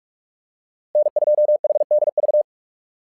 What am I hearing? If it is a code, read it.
Morse code, heard as a keyed tone, about 44 wpm, 600 Hz. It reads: N2HDV
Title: SOTA QSO Practice 45wpm